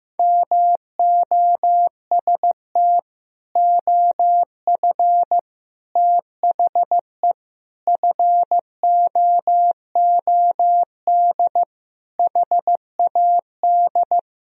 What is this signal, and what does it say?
Morse code, heard as a keyed tone, about 15 wpm, 700 Hz. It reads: MOST OF THE FOOD HAD
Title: Sentences from Top 1000 Words - Review Part 1 - 15wpm